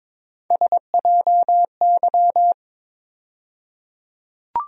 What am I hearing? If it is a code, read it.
Morse code, heard as a keyed tone, about 22 wpm, 700 Hz. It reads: SJY
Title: USA Call Sign Postfixes - 3 Letters 22wpm